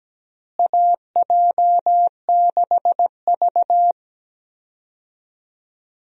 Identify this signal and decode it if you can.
Morse code, heard as a keyed tone, about 17 wpm, 700 Hz. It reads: AJ6V
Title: USA Call Signs - 4 Characters - Repeated 3x - 17wpm